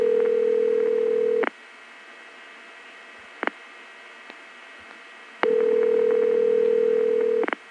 Ringback tone (North America), 440 + 480 Hz.